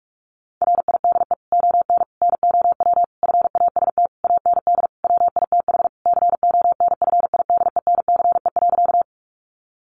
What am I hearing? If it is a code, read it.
Morse code, heard as a keyed tone, about 44 wpm, 700 Hz. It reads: RIDE ON NOW FAST AND WITH CONFIDENCE.
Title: Sentences from Top 1500 Words 45wpm